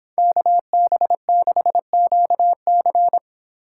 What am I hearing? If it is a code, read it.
Morse code, heard as a keyed tone, about 26 wpm, 700 Hz. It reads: KB6QC